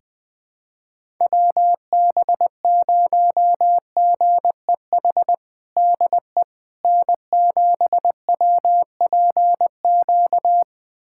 Morse code, heard as a keyed tone, about 20 wpm, 700 Hz. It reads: WB0GEH DE N7WPQ
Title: USA Callsign DE Callsign 20wpm